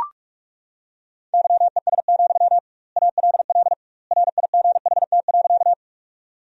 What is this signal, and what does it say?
Morse code, heard as a keyed tone, about 42 wpm, 700 Hz. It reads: YES, ALL RIGHT.
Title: 2x Word-Spacing - Sentences from Top 500 Words - 45wpm